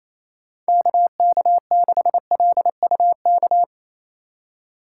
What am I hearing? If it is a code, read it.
Morse code, heard as a keyed tone, about 28 wpm, 700 Hz. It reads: KK6LUK
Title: USA Call Signs - 6 Characters - 4x Character Spacing - 28wpm